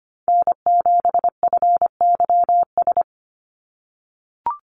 Morse code, heard as a keyed tone, about 25 wpm, 700 Hz. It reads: N7FYS
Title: USA Call Signs 25wpm